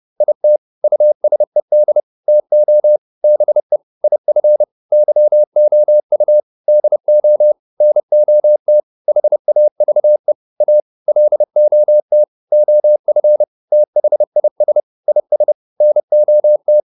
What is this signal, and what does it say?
Morse code, heard as a keyed tone, about 30 wpm, 600 Hz. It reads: IT USED TO BE IF YOU DO NOT HAVE A LOT OF THIS IS NOT